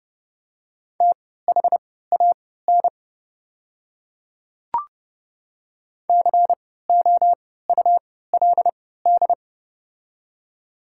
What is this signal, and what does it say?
Morse code, heard as a keyed tone, about 30 wpm, spaced as for 17 wpm, 690 Hz. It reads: THAN COULD
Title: 3x Character Spacing - Top 100 Words - 30wpm